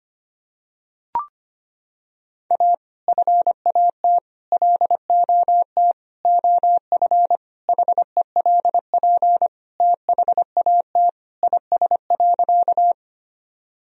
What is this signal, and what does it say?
Morse code, heard as a keyed tone, about 25 wpm, 700 Hz. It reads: A FAT LOT OF HELP THAT IS.
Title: Sentences from Top 1000 Words - Review Part 4 - 25wpm